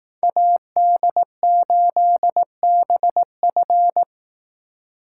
Morse code, heard as a keyed tone, about 18 wpm, 700 Hz. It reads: AD8BF
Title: Speed Racing - USA 2x2 Call Signs 15wpm